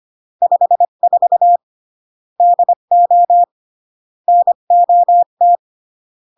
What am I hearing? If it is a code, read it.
Morse code, heard as a keyed tone, about 25 wpm, 700 Hz. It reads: 54 DO NOT
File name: CW / jodorowsky / 82 commandments 25-20